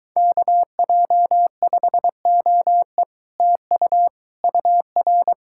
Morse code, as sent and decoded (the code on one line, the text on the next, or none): -.- .--- ..... --- . / - ..- / ..- .-.
KJ5OE TU UR